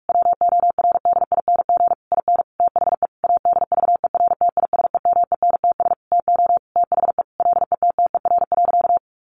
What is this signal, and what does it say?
Morse code, heard as a keyed tone, about 44 wpm, 700 Hz. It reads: WORDING IN THE ADVERTISEMENTS TO THE LETTER.
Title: A Sound Of Thunder 45wpm